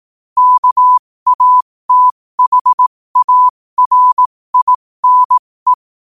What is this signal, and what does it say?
Morse code, heard as a keyed tone, about 18 wpm, spaced as for 15 wpm, 1000 Hz. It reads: KATHARINE
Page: qotd-at-15-wpm